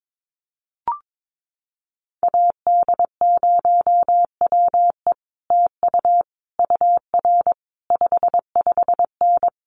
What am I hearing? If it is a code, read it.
Morse code, heard as a keyed tone, about 22 wpm, 700 Hz. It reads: AD0WE TU UR 55N
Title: Common QSO Phrases Encore 22wpm